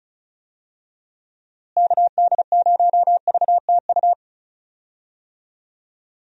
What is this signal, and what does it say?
Morse code, heard as a keyed tone, about 36 wpm, 700 Hz. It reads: KD0VTU
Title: USA Call Signs - 6 Characters - Repeated 3x - 35wpm